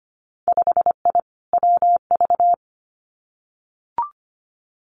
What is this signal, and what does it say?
Morse code, heard as a keyed tone, about 25 wpm, 700 Hz. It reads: 5I WV
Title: Winter Field Day Exchanges Encore 25wpm